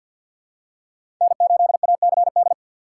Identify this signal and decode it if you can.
Morse code, heard as a keyed tone, about 52 wpm, 700 Hz. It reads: N8ACD
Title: USA Call Signs - 4x Character Spacing - 50wpm